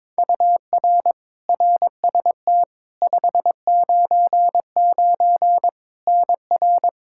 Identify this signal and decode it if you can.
Morse code, heard as a keyed tone, about 22 wpm, 700 Hz. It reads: UR RST 599 NR
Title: K1USN SST QSO Practice 22wpm